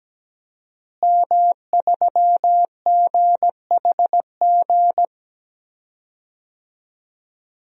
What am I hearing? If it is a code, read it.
Morse code, heard as a keyed tone, about 17 wpm, 700 Hz. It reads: M3GHG
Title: DXCC-100 Call Signs - 5 Characters - Repeated 2x - 17wpm